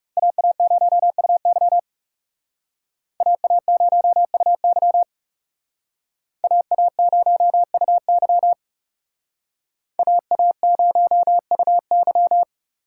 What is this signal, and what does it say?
Morse code, heard as a keyed tone, about 38 wpm, 700 Hz. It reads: AA0UY AA0UY AA0UY AA0UY